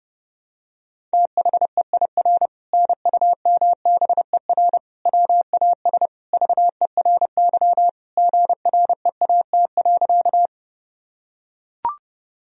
Morse code, heard as a keyed tone, about 30 wpm, 700 Hz. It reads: THEIR NUMBER WAS VERY GREAT.
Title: Sentences from Top 1000 Words - Review Part 1 - 30wpm